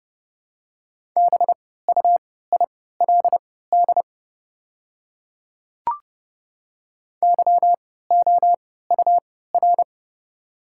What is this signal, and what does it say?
Morse code, heard as a keyed tone, about 32 wpm, spaced as for 17 wpm, 690 Hz. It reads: BUILD YOUR
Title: 3x Character Spacing - Top 200 Words - 30wpm